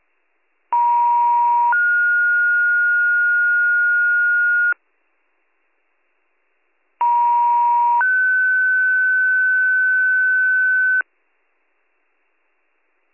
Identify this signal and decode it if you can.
Two-tone page, heard as a one-second tone, then a three-second tone, about 950 then 1430 Hz, about 950 then 1500 Hz. It is heard from 0.7 to 11.0 s.